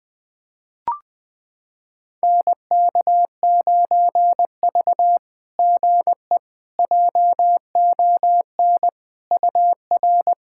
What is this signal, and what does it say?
Morse code, heard as a keyed tone, about 21 wpm, 700 Hz. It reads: NK9V GE JON UR
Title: SOTA QSO Practice 20wpm